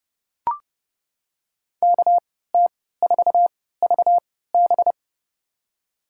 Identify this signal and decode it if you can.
Morse code, heard as a keyed tone, about 31 wpm, spaced as for 17 wpm, 700 Hz. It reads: KT4VB
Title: USA Call Signs - 3x Character Spacing - 30wpm